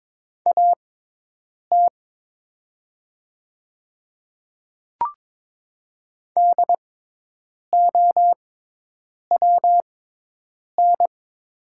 Morse code, heard as a keyed tone, about 22 wpm, spaced as for 8 wpm, 700 Hz. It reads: AT DOWN
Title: 6x Character Spacing - Top 200 Words - 22wpm